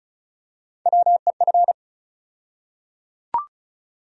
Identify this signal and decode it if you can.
Morse code, heard as a keyed tone, about 34 wpm, 700 Hz. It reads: WEF
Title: USA Call Sign Postfixes - 3 Letters 35wpm